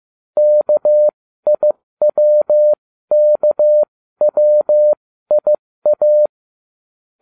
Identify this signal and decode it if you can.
Morse code, heard as a keyed tone, about 15 wpm, spaced as for 13 wpm, 600 Hz. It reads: KIWKWIA